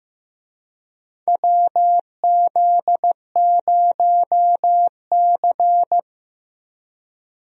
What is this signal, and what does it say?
Morse code, heard as a keyed tone, about 15 wpm, 700 Hz. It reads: WZ0C
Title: USA Call Signs - 4 Characters - 15wpm